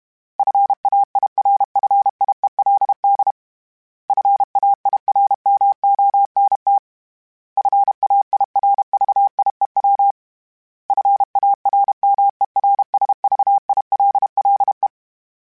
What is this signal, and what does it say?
Morse code, heard as a keyed tone, about 32 wpm, 800 Hz. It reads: FAIRFIELD FAIRMONT FAIRVIEW FARMERSVILLE
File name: fromSMs / FromSMs mp3s / Cities@30